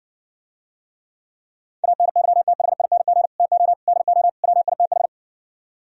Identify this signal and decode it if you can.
Morse code, heard as a keyed tone, about 60 wpm, 700 Hz. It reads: ANYTHING TO DO WITH